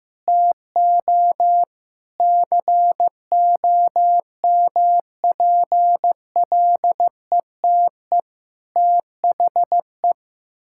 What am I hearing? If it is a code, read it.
Morse code, heard as a keyed tone, about 15 wpm, 700 Hz. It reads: TO COMPLETE THE
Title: Sentences from Top 1000 Words - Review Part 6 - 15wpm